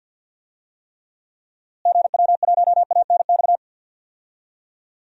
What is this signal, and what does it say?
Morse code, heard as a keyed tone, about 52 wpm, 700 Hz. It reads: GW1ANX
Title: DXCC-100 Call Signs - 6 Characters - Repeated 3x - 50wpm